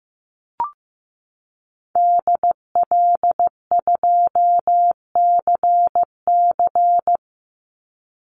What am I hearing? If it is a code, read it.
Morse code, heard as a keyed tone, about 15 wpm, 700 Hz. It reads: DL2CC
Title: Call Signs Repeated 2x - 15wpm